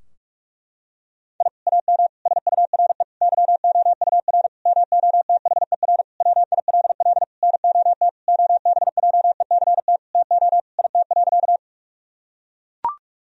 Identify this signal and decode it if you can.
Morse code, heard as a keyed tone, about 41 wpm, 700 Hz. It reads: I AM SURE YOUR MOTHER WILL NOT OBJECT TO IT.
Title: Sentences from Top 1000 Words - Review Part 3 - 45wpm